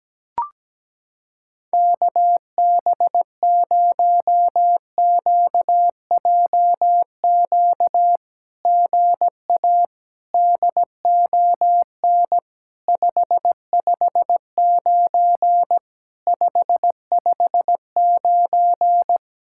Morse code, heard as a keyed tone, about 17 wpm, 700 Hz. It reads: KB0QJQ GA DON 559 559
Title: SOTA QSO Practice 17wpm